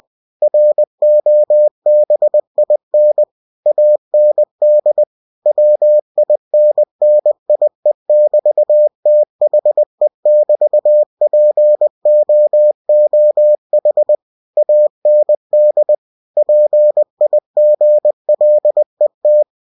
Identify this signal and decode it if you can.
Morse code, heard as a keyed tone, about 20 wpm, 600 Hz. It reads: ROBIN AND WINNIE<BT>THE<BT>POOH AND PIGLET
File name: cw-books / winnie-the-pooh / winnie-20wpm-page-0033